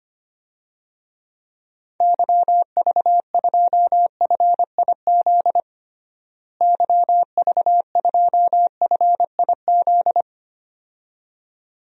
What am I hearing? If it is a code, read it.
Morse code, heard as a keyed tone, about 25 wpm, 700 Hz. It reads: YV2FIZ YV2FIZ
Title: DXCC-100 Call Signs - 6 Characters - Repeated 2x - 25wpm